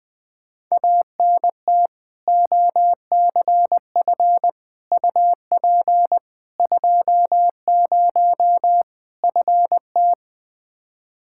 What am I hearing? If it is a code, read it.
Morse code, heard as a keyed tone, about 20 wpm, 700 Hz. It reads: ANT OCF UP 20 FT
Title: Practice QSO - Exchange 2 - Info - 20wpm